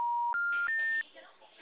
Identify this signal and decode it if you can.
Special information tone, with speech.
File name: out- -2017- - - .6143